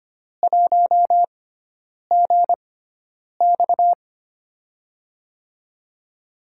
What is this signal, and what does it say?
Morse code, heard as a keyed tone, about 25 wpm, spaced as for 9 wpm, 700 Hz. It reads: 1GX
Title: USA Call Signs - 6x Character Spacing - 25wpm